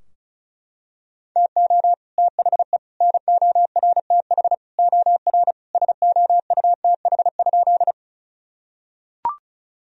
Morse code, heard as a keyed tone, about 36 wpm, 700 Hz. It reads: TO THE NORTH OR SOUTH?
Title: Sentences from Top 1000 Words - Review Part 2 - 35wpm